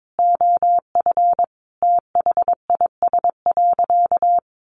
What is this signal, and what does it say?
Morse code, heard as a keyed tone, about 22 wpm, 700 Hz. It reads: OF THIS.
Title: Sentences from Top 400 Words 22wpm